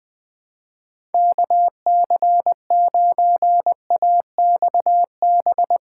Morse code, heard as a keyed tone, about 20 wpm, 700 Hz. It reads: KC9AXB
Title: Speed-Racing Common QSO Phrases - 17wpm